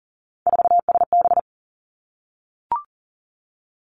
Morse code, heard as a keyed tone, about 40 wpm, 700 Hz. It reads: 4SB
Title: Three Challenging Letter-Number Combinations 40wpm